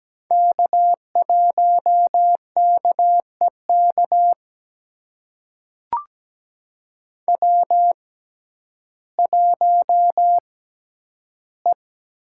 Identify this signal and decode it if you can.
Morse code, heard as a keyed tone, about 17 wpm, spaced as for 10 wpm, 700 Hz. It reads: K1KEK W1E
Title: USA Call Signs - 5 Characters - 6x Character Spacing - 17wpm